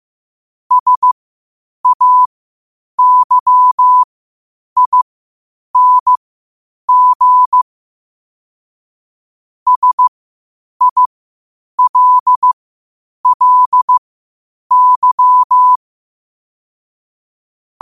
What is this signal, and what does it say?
Morse code, heard as a keyed tone, about 15 wpm, spaced as for 8 wpm, 1000 Hz. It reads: SAYING SILLY
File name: qotdfiles / 8wpm